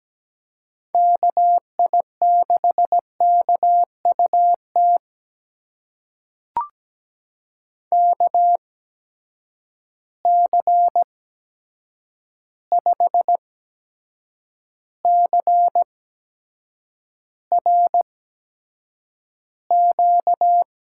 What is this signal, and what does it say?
Morse code, heard as a keyed tone, about 17 wpm, spaced as for 7 wpm, 700 Hz. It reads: KI6KUT KC5CRQ
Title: USA Call Signs - 6 Characters - 8x Character Spacing - 17wpm